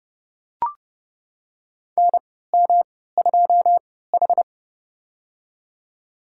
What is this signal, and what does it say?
Morse code, heard as a keyed tone, about 30 wpm, spaced as for 17 wpm, 700 Hz. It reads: NM2H
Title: USA Call Signs - 3x Character Spacing - 30wpm